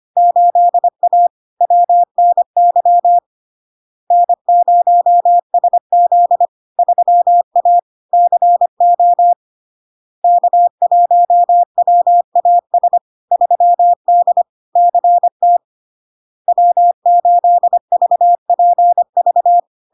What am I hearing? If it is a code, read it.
Morse code, heard as a keyed tone, about 25 wpm, 700 Hz. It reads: 8A WNY N0SZ 3A CO K1WAS 3D CT W8VPV
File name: LARK-FD-2022-Log-in-CW-25wpm-700-Hz